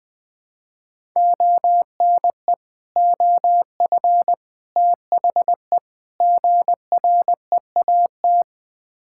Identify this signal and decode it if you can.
Morse code, heard as a keyed tone, about 20 wpm, 700 Hz. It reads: ONE OF THE GREAT
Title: Build A Phrase - 4-Word Phrases - 20wpm